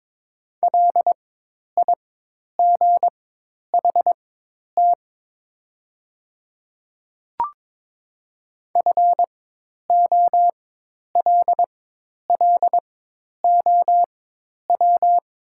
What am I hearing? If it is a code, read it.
Morse code, heard as a keyed tone, about 22 wpm, spaced as for 10 wpm, 700 Hz. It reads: LIGHT FOLLOW
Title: 4x Character Spacing - Top 300 Words - 22wpm